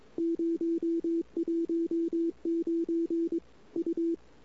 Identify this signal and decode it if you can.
Morse code, heard as a keyed tone, about 21 wpm, 340 Hz. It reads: 019 U
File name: CW / 2008-ss-cw-k5tr-op-n5rz -23:00